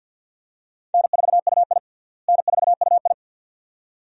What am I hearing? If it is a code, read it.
Morse code, heard as a keyed tone, about 49 wpm, 700 Hz. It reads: N4UI N4UI
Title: USA Call Signs - 4 Characters - Repeated 2x - 50wpm